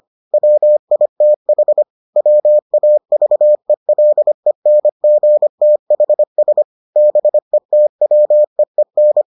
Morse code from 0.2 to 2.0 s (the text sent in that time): WITH